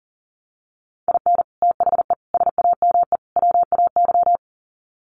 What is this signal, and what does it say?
Morse code, heard as a keyed tone, about 40 wpm, 700 Hz. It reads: IN THE SAME WAY
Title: Sets of 4 Words 40wpm